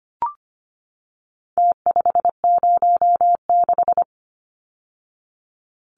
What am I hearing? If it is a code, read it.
Morse code, heard as a keyed tone, about 25 wpm, 700 Hz. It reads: T506
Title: Numbers - Contest Style - 4-Digit 25wpm